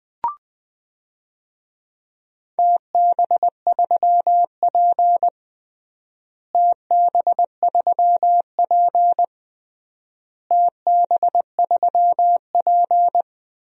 Morse code, heard as a keyed tone, about 21 wpm, 700 Hz. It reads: TB3P TB3P TB3P
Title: DXCC-100 Call Signs - 4 Characters - Repeated 3x - 20wpm